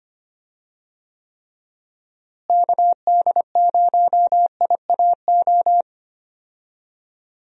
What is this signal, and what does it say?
Morse code, heard as a keyed tone, about 25 wpm, 700 Hz. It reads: KD0IAO
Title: USA Call Signs - 6 Characters - Repeated 3x - 25wpm